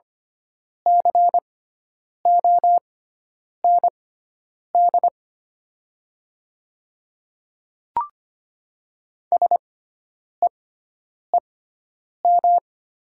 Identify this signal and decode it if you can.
Morse code, heard as a keyed tone, about 25 wpm, spaced as for 9 wpm, 700 Hz. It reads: COND SEEM
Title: 6x Character Spacing - Top 300 Words - 25wpm